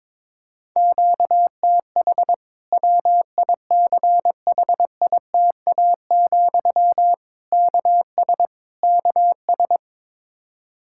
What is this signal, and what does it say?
Morse code, heard as a keyed tone, about 22 wpm, 700 Hz. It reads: QTH WICHITA, KS KS
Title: Practice QSO - Exchange 1 - Beginning - 22wpm